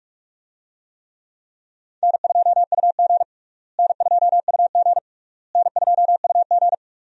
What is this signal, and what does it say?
Morse code, heard as a keyed tone, about 41 wpm, 700 Hz. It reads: N2UG N2UG N2UG
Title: USA Call Signs - 4 Characters - Repeated 3x - 45wpm